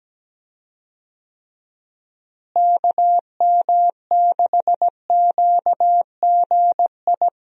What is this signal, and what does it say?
Morse code, heard as a keyed tone, about 17 wpm, 700 Hz. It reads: KM6QGI
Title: USA Call Signs - 6 Characters - Repeated 2x - 17wpm